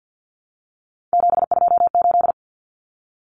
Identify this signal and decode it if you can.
Morse code, heard as a keyed tone, about 48 wpm, 700 Hz. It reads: 728
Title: Common Cut Numbers - 3-Digit 50wpm